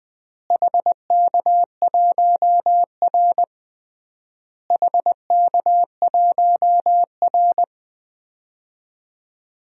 Morse code, heard as a keyed tone, about 20 wpm, 700 Hz. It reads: HK1R HK1R
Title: DXCC-100 Call Signs - 4 Characters - Repeated 3x - 20wpm